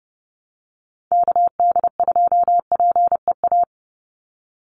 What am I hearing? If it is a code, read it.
Morse code, heard as a keyed tone, about 30 wpm, 700 Hz. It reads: KD2PEA
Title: USA Call Signs 30wpm